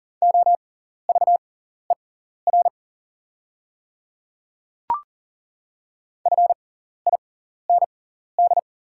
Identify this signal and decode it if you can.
Morse code, heard as a keyed tone, about 39 wpm, spaced as for 14 wpm, 690 Hz. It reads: OVER FIND
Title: 6x Character Spacing - Top 100 Words - 40wpm